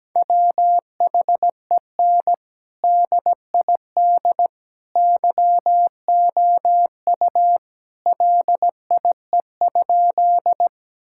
Morse code, heard as a keyed tone, about 17 wpm, 700 Hz. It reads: WHEN DID YOU LIE?
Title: Sentences from Top 1600 Words 17wpm